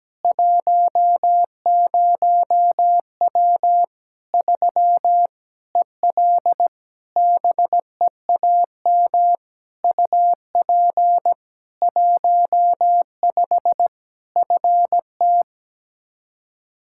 Morse code, heard as a keyed tone, about 17 wpm, 700 Hz. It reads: 10W 3 EL BEAM UP 15 FT
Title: Practice QSO - Exchange 2 - Info - 17wpm